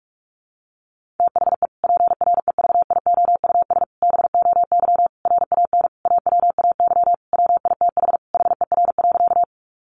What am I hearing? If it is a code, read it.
Morse code, heard as a keyed tone, about 46 wpm, 700 Hz. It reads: THE PREVIOUS BOY RAN AWAY WITH HER.
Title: Sentences from Top 1500 Words 45wpm